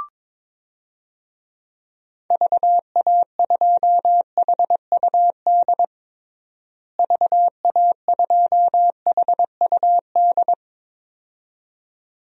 Morse code, heard as a keyed tone, about 22 wpm, 700 Hz. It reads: VA2HUD VA2HUD
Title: DXCC-100 Call Signs - 6 Characters - Repeated 2x - 22wpm